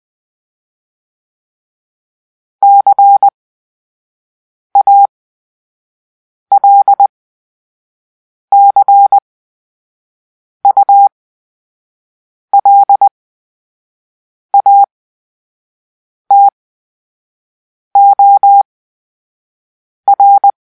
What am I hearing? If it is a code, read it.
Morse code, heard as a keyed tone, about 20 wpm, spaced as for 5 wpm, 800 Hz. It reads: CALCULATOR
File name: Grp-3 Drill-4